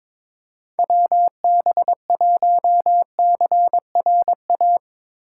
Morse code, heard as a keyed tone, about 22 wpm, 700 Hz. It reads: WB1CRA
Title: USA Call Signs - 6 Characters - 8x Character Spacing - 22wpm